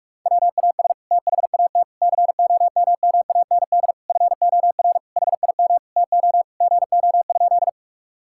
Morse code, heard as a keyed tone, about 46 wpm, 700 Hz. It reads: WAS THAT COMMAND FOR HIM TO GO?
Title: Sentences from Top 1600 Words 45wpm